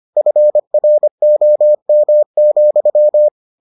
Morse code, heard as a keyed tone, about 25 wpm, 600 Hz. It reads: FROM,